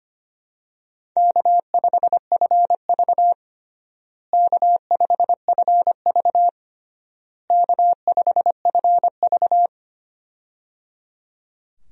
Morse code, heard as a keyed tone, about 25 wpm, 700 Hz. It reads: K5FV K5FV K5FV
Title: USA Call Signs - 4 Characters - Repeated 3x - 25wpm